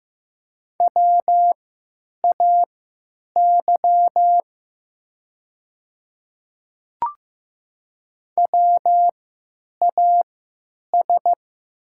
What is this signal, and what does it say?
Morse code, heard as a keyed tone, about 15 wpm, spaced as for 9 wpm, 700 Hz. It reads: WAY WAS
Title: 3x Character Spacing - Top 100 Words - 15wpm